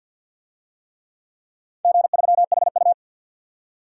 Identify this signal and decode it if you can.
Morse code, heard as a keyed tone, about 48 wpm, 700 Hz. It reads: G3HU
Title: DXCC-100 Call Signs - 4 Characters - 50wpm